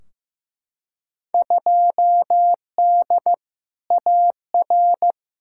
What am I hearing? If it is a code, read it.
Morse code, heard as a keyed tone, about 15 wpm, 700 Hz. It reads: 2D AR
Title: Field Day Exchanges Encore - 15wpm